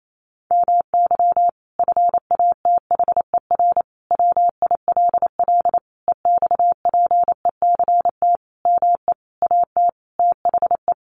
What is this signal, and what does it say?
Morse code, heard as a keyed tone, about 28 wpm, 700 Hz. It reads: MY FATHER WILL EXPECT ME AT THE
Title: Sentences from Top 1100 Words 28wpm